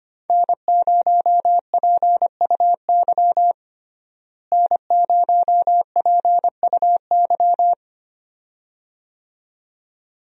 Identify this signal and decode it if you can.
Morse code, heard as a keyed tone, about 25 wpm, 700 Hz. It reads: N0PUY N0PUY
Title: USA Call Signs - 5 Characters - Repeated 2x - 25wpm